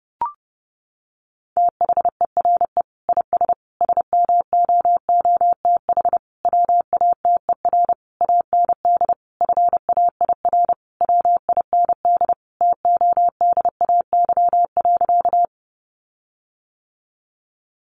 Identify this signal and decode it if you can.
Morse code, heard as a keyed tone, about 30 wpm, 700 Hz. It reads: THERE IS SMOOTH WATER AND FAIR WIND TODAY.
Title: Sentences from Top 1400 Words 30wpm